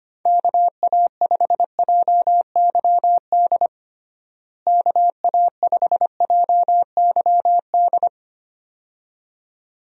Morse code, heard as a keyed tone, about 25 wpm, 700 Hz. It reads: KA5JYD KA5JYD
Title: USA Call Signs - 6 Characters - Repeated 2x - 25wpm